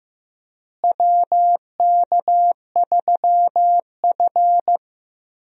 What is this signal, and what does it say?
Morse code, heard as a keyed tone, about 15 wpm, 700 Hz. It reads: WK3F
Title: USA Call Signs - 4 Characters - Repeated 2x - 15wpm